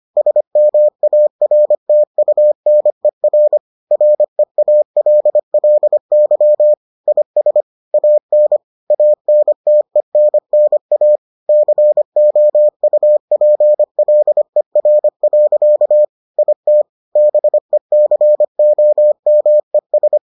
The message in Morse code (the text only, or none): SMARTUNER REALLY IS AN ANTENNA COUPLER. IT BECOMES